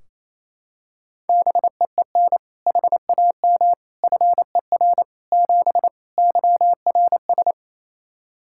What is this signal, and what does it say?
Morse code, heard as a keyed tone, about 28 wpm, 700 Hz. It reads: BEEN HAM FER 7 YRS
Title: K1USN SST QSO Practice 28wpm